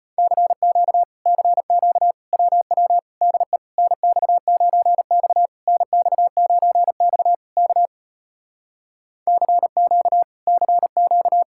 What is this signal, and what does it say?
Morse code, heard as a keyed tone, about 33 wpm, 700 Hz. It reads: CQ CQ WW DE NX9X NX9X K CQ CQ